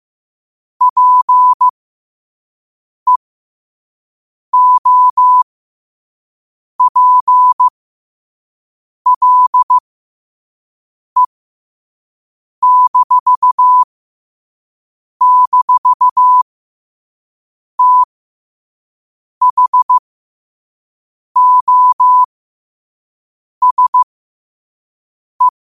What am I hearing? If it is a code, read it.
Morse code, heard as a keyed tone, about 15 wpm, spaced as for 5 wpm, 1000 Hz. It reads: PEOPLE--THOSE
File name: qotdfiles / 5wpm